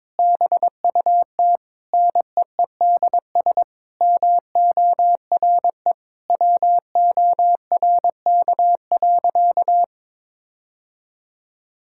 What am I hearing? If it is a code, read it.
Morse code, heard as a keyed tone, about 22 wpm, 700 Hz. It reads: BUT NEEDS MORE WORK.
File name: practice / 22wpm / Sentences with 8-Word Phrases - 22wpm